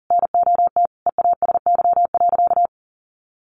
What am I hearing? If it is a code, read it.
Morse code, heard as a keyed tone, about 40 wpm, 700 Hz. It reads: NOT EASY.
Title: Sentences from Top 1500 Words 40wpm